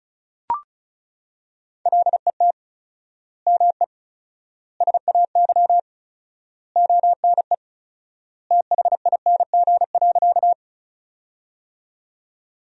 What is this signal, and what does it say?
Morse code, heard as a keyed tone, about 34 wpm, 700 Hz. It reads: LET ME SAY ONE THING.
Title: 4x Word-Spacing - Sentences from Top 300 Words - 35wpm